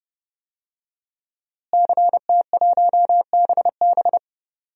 Morse code, heard as a keyed tone, about 30 wpm, 700 Hz. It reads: CT1BB
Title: DXCC-100 Call Signs - 5 Characters - Repeated 2x - 30wpm